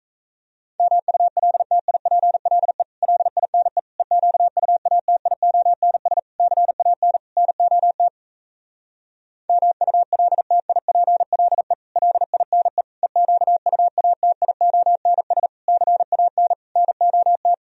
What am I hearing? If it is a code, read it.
Morse code, heard as a keyed tone, about 37 wpm, 700 Hz. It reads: MULTIPLE LINE EQUATIONS CAN NOT MULTIPLE LINE EQUATIONS CAN NOT